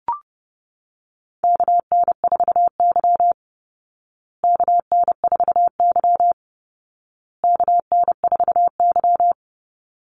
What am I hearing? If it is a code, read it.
Morse code, heard as a keyed tone, about 30 wpm, 700 Hz. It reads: KN4Y KN4Y KN4Y
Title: Call Signs Repeated 2x - 30wpm